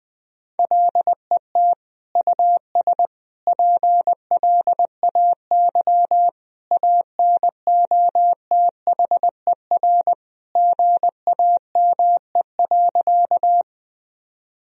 Morse code, heard as a keyed tone, about 21 wpm, 700 Hz. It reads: LET US PLAY ANOTHER GAME.
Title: Sentences from Top 1000 Words - Review Part 3 - 20wpm